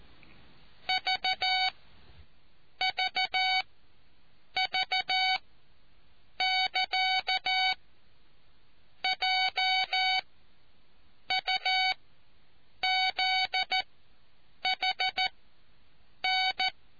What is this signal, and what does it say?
Morse code, heard as a keyed tone, about 14 wpm, spaced as for 7 wpm, 750 Hz. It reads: VVV <KA> J UZHN